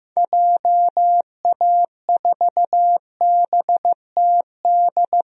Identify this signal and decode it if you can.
Morse code, heard as a keyed tone, about 15 wpm, 700 Hz. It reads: JA4BTD
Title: DXCC-100 Call Signs - 6 Characters - 15wpm